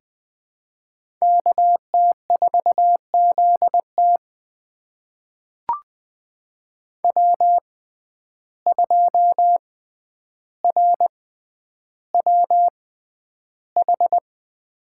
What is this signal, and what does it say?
Morse code, heard as a keyed tone, about 20 wpm, spaced as for 10 wpm, 700 Hz. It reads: KT4ZT W2RWH
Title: USA Call Signs - 5 Characters - 6x Character Spacing - 20wpm